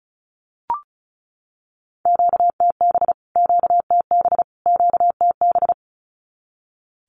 Morse code, heard as a keyed tone, about 34 wpm, 700 Hz. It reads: QTB QTB QTB
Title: Q-Codes Repeated 3x 35wpm